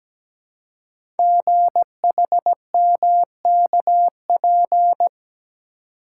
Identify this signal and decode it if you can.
Morse code, heard as a keyed tone, about 17 wpm, 700 Hz. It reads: GHMKP
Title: Code Groups - 5 Letters - 17wpm